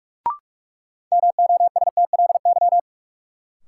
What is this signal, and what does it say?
Morse code, heard as a keyed tone, about 46 wpm, 700 Hz. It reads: MOSTLY
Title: Sentences from Top 1300 Words - New Word Drill Alt - 45wpm